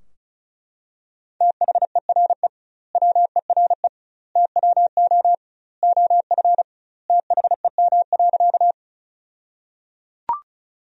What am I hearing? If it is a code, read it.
Morse code, heard as a keyed tone, about 36 wpm, 700 Hz. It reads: THERE WERE TWO OF THEM.
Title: 2x Word-Spacing - Sentences from Top 100 Words - 35wpm